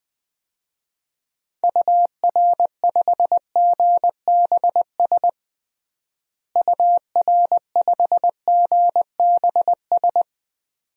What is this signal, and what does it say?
Morse code, heard as a keyed tone, about 20 wpm, 700 Hz. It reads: UR5GBS UR5GBS
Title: DXCC-100 Call Signs - 6 Characters - Repeated 2x - 20wpm